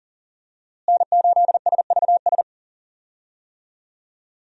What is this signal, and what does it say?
Morse code, heard as a keyed tone, about 37 wpm, 700 Hz. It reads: N8SVS
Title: USA Call Signs - 5 Characters - Repeated 2x - 40wpm